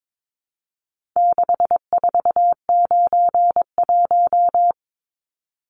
Morse code, heard as a keyed tone, about 22 wpm, 700 Hz. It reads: 6491
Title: Numbers - 4-Digit 22wpm